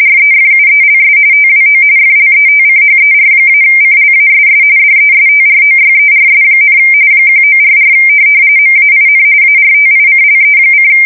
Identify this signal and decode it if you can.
RTTY, about 45.45 baud. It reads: REFUELS TWO FLETCHER-CLASS DESTROYERS, 1944 MAIN ARTICLE:
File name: fsk170 / USS-Massachusetts